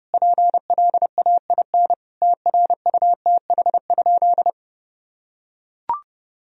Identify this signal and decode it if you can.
Morse code, heard as a keyed tone, about 30 wpm, 700 Hz. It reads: PLAIN TRUTH?
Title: Sentences from Top 1000 Words - Review Part 2 - 30wpm